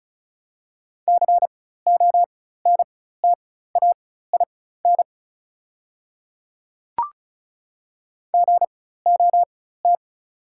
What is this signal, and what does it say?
Morse code, heard as a keyed tone, about 34 wpm, spaced as for 16 wpm, 690 Hz. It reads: CONTAIN GOT
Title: 4x Character Spacing - Top 500 Words - 35wpm